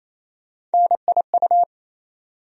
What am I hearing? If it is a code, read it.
Morse code, heard as a keyed tone, about 27 wpm, 700 Hz. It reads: NIU
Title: USA Call Sign Postfixes - 3 Letters 28wpm